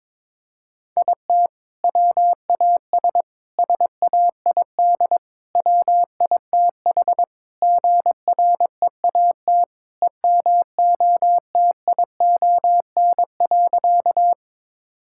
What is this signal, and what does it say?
Morse code, heard as a keyed tone, about 22 wpm, 700 Hz. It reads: IT WAS SAID WITH GREAT EMOTION.
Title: Sentences from Top 1600 Words 22wpm